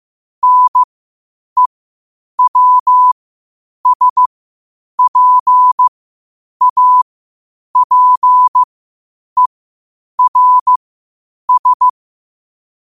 Morse code, heard as a keyed tone, about 15 wpm, spaced as for 8 wpm, 1000 Hz. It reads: NEWSPAPERS
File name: qotdfiles / 8wpm